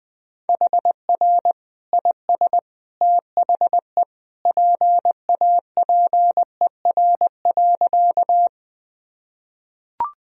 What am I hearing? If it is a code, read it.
Morse code, heard as a keyed tone, about 21 wpm, 700 Hz. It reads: HR IS THE PAPER.
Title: Sentences from Top 300 Words and Abbreviations 20wpm